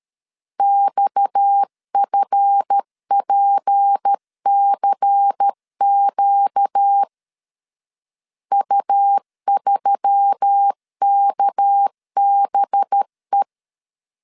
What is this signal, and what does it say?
Morse code, heard as a keyed tone, about 13 wpm, 800 Hz. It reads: XFPCQ U3KBE
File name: LEZIONI / 7 GF9Z / 7 GF 12wpm 2122